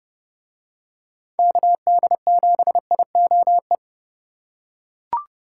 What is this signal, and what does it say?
Morse code, heard as a keyed tone, about 30 wpm, 700 Hz. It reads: KD7IOE